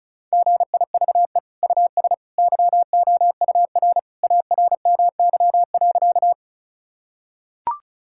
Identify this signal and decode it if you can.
Morse code, heard as a keyed tone, about 36 wpm, 700 Hz. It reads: GIVE US YOUR ARMY.
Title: Sentences from Top 1000 Words - Review Part 5 - 35wpm